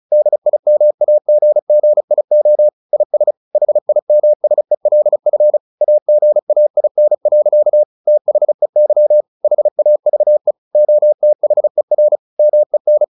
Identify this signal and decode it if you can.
Morse code, heard as a keyed tone, about 34 wpm, 600 Hz. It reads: DIMAGGIO IS HIMSELF AGAIN. THEY HAVE OTHER MEN
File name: the old man and the sea / the old man and the sea 1 35wpm